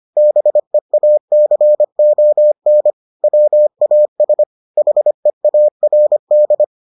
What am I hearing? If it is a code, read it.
Morse code, heard as a keyed tone, about 25 wpm, 600 Hz. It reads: BEACON WAS HEARD